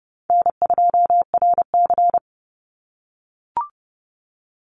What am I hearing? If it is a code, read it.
Morse code, heard as a keyed tone, about 31 wpm, 700 Hz. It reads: N2RC
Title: Easy Call Signs - 30wpm